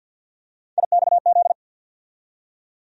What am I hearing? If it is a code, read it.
Morse code, heard as a keyed tone, about 51 wpm, 700 Hz. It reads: IXZ
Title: USA Call Sign Postfixes - 3 Letters 50wpm